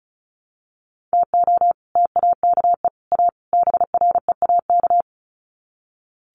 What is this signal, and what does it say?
Morse code, heard as a keyed tone, about 34 wpm, 700 Hz. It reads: TO TAKE A BREAK
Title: Sets of 4 Words 35wpm